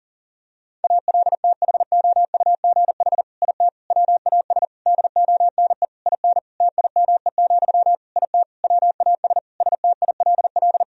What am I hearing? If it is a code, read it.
Morse code, heard as a keyed tone, about 40 wpm, 700 Hz. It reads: ALTHOUGH IT WAS DONE IN TIME, IT WAS STILL
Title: Sentences from Top 1000 Words - Review Part 3 - 40wpm